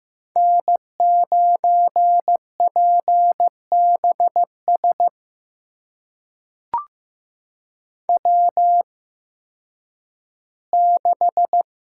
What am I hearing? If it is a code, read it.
Morse code, heard as a keyed tone, about 15 wpm, spaced as for 10 wpm, 700 Hz. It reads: N9PBS W6
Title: USA Call Signs - 5 Characters - 8x Character Spacing - 15wpm